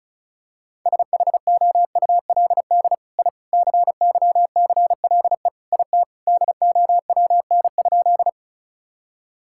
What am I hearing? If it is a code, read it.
Morse code, heard as a keyed tone, about 36 wpm, 700 Hz. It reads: SHOULD I CYCLE IT DOWN?
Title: Sentences from Top 1000 Words - Review Part 6 - 35wpm